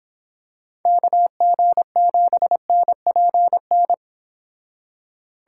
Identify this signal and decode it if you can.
Morse code, heard as a keyed tone, about 26 wpm, 700 Hz. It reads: KG7NPN